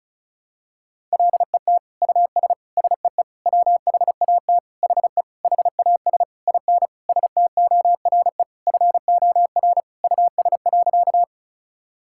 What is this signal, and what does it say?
Morse code, heard as a keyed tone, about 34 wpm, 700 Hz. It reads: LET US SEE WHAT HE HAS IN STORE FOR US.
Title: Sentences from Top 1000 Words - Review Part 5 - 35wpm